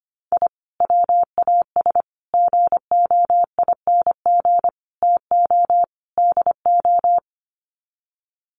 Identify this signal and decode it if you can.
Morse code, heard as a keyed tone, about 25 wpm, 700 Hz. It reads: I WAS GOING TO DO
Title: Sets of 5 Words Encore2 25wpm